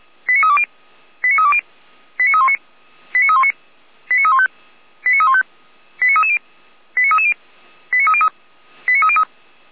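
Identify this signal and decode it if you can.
Five-tone selective call (ZVEI), heard as quick runs of five tones, tones of about 70 ms. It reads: 89320 89320 89319 89319 89315 89315 89330 89330 89303 89303
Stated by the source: ZVEI five-tone call received by SDR.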